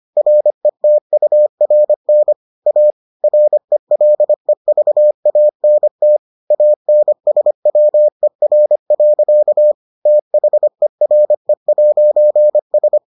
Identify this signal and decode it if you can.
Morse code, heard as a keyed tone, about 25 wpm, 600 Hz. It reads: RETURN A RELEVANT ANSWER. THERE'S